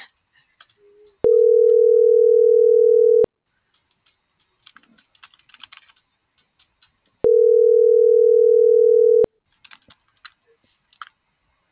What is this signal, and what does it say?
Ringback tone (North America), 440 + 480 Hz.
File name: CENTRAL TERRESTRE / out karis